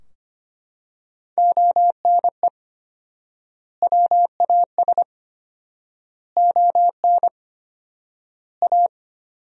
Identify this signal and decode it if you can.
Morse code, heard as a keyed tone, about 25 wpm, 700 Hz. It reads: ONE WAS ON A
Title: 4x Word-Spacing - Sentences from Top 300 Words - 25wpm